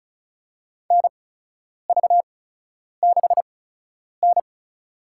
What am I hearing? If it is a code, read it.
Morse code, heard as a keyed tone, about 34 wpm, spaced as for 10 wpm, 700 Hz. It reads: NV6N
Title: USA Call Signs - 4 Characters - 8x Character Spacing - 35wpm